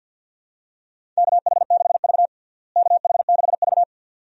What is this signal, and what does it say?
Morse code, heard as a keyed tone, about 53 wpm, 700 Hz. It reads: KH6V KH6V
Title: DXCC-100 Call Signs - 4 Characters - Repeated 2x - 50wpm